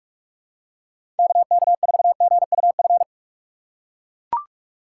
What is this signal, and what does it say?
Morse code, heard as a keyed tone, about 44 wpm, 700 Hz. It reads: KK4GUF
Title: USA Call Signs - 6 Characters - 6x Character Spacing - 45wpm